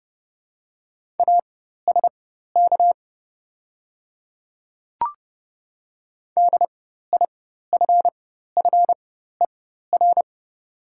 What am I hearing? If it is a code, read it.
Morse code, heard as a keyed tone, about 30 wpm, spaced as for 14 wpm, 700 Hz. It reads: ASK DIFFER
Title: 4x Character Spacing - Top 200 Words - 30wpm